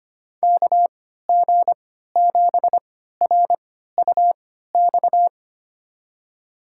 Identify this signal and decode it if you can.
Morse code, heard as a keyed tone, about 25 wpm, spaced as for 14 wpm, 700 Hz. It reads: KG7RUX
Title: USA Call Signs - 6 Characters - 3x Character Spacing - 25wpm